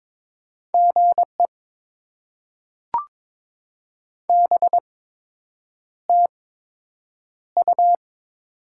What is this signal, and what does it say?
Morse code, heard as a keyed tone, about 22 wpm, spaced as for 8 wpm, 700 Hz. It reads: GE BTU
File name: Top 20 QSO Elements - 8x Character Spacing - 22wpm